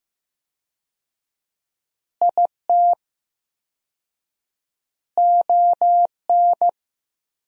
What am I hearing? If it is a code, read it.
Morse code, heard as a keyed tone, about 15 wpm, 700 Hz. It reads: IT ON
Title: 4x Word-Spacing - Sentences from Top 400 Words - 15wpm